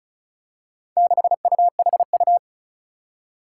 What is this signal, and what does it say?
Morse code, heard as a keyed tone, about 35 wpm, 700 Hz. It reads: 6UHU
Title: Four Challenging Letter-Number Combinations 35wpm